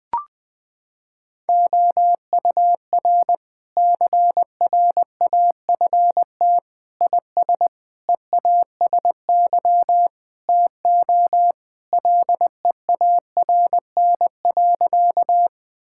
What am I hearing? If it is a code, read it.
Morse code, heard as a keyed tone, about 20 wpm, 700 Hz. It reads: OUR CRAFT IS EASY TO LEARN.
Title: Sentences from Top 1000 Words - Review Part 6 - 20wpm